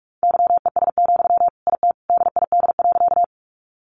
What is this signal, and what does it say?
Morse code, heard as a keyed tone, about 44 wpm, 700 Hz. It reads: YES, IT DID.
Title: Sentences from Top 500 Words 45wpm